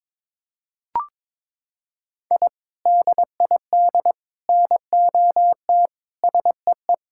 Morse code, heard as a keyed tone, about 22 wpm, 700 Hz. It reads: I DID NOT SEE
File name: practice / 22wpm / Sentences with 3-Word Phrases - 22wpm